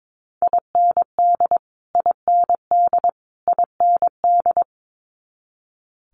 Morse code, heard as a keyed tone, about 22 wpm, 700 Hz. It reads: IND IND IND
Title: Multi-letter Phonemes Repeated 3x 22wpm